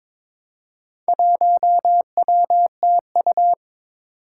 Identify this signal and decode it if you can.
Morse code, heard as a keyed tone, about 22 wpm, 700 Hz. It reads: 1WTU
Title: Four Challenging Letter-Number Combinations 22wpm